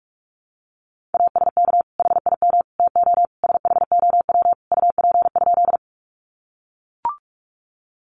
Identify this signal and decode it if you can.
Morse code, heard as a keyed tone, about 44 wpm, 700 Hz. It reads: ASK HIM TO SHOW UP?
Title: Sentences from Top 200 Words 45wpm